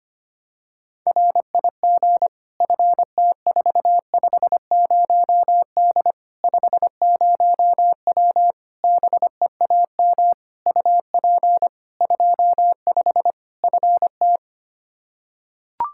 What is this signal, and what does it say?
Morse code, heard as a keyed tone, about 25 wpm, 700 Hz. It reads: RIG FT450D 50W BEAM UP 25 FT
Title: Practice QSO - Exchange 2 - Info - 25wpm